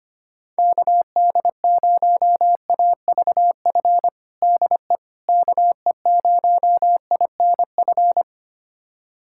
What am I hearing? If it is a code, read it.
Morse code, heard as a keyed tone, about 25 wpm, 700 Hz. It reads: KD0AVF DE KE0INF
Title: USA Callsign DE Callsign 25wpm